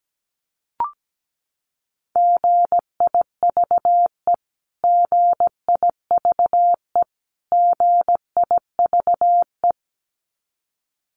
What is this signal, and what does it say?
Morse code, heard as a keyed tone, about 17 wpm, 700 Hz. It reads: GIVE GIVE GIVE
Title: Top 500 Words Repeated 17wpm